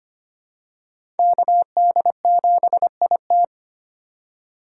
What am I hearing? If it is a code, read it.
Morse code, heard as a keyed tone, about 25 wpm, 700 Hz. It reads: KD7IT
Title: USA Call Signs - 5 Characters - 2x Character Spacing - 25wpm